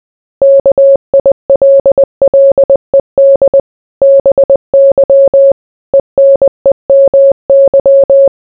Morse code, heard as a keyed tone, about 20 wpm, 550 Hz. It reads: KILLED BY ENEMY